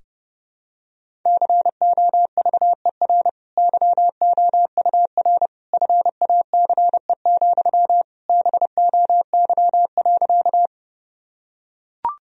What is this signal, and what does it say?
Morse code, heard as a keyed tone, about 30 wpm, 700 Hz. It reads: COVER YOUR FACE, BOY.
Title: Sentences from Top 1000 Words - Review Part 1 - 30wpm